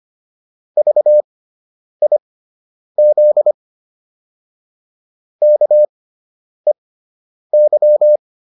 Morse code, heard as a keyed tone, about 25 wpm, spaced as for 9 wpm, 600 Hz. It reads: VIZ KEY